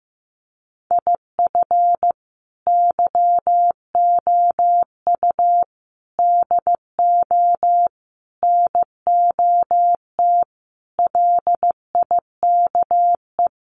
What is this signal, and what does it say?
Morse code, heard as a keyed tone, about 15 wpm, 700 Hz. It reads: IF YOU DO NOT LIKE
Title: Sets of 5 Words 15wpm